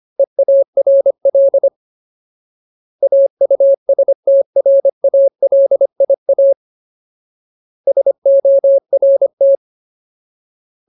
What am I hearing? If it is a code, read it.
Morse code, heard as a keyed tone, about 25 wpm, 550 Hz. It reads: EARL AUSTRALIA SORT
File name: Ultimate Word List / 25WPM Ultimate